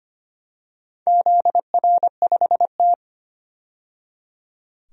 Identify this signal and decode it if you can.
Morse code, heard as a keyed tone, about 25 wpm, 700 Hz. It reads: ZR5T
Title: DXCC-100 Call Signs - 4 Characters - 25wpm